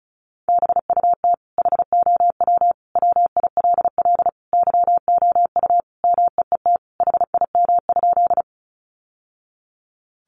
Morse code, heard as a keyed tone, about 34 wpm, 700 Hz. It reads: BUT HOW WILL YOU MEET HIM?
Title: Sentences from Top 1100 Words 35wpm